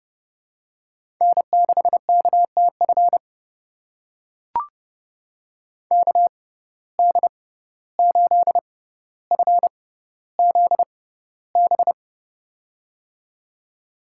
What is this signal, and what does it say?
Morse code, heard as a keyed tone, about 30 wpm, spaced as for 15 wpm, 700 Hz. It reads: N6KTF KD8FZB
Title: USA Call Signs - 6x Character Spacing - 30wpm